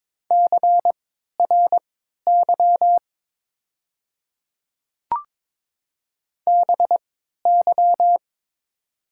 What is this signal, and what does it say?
Morse code, heard as a keyed tone, about 22 wpm, spaced as for 12 wpm, 690 Hz. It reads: CRY BY
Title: 3x Character Spacing - Top 500 Words - 22wpm